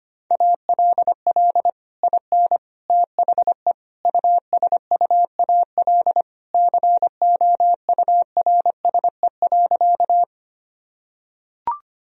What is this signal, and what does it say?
Morse code, heard as a keyed tone, about 25 wpm, 700 Hz. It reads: ALL IN THE USUAL COURSE.
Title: Sentences from Top 1000 Words - Review Part 2 - 25wpm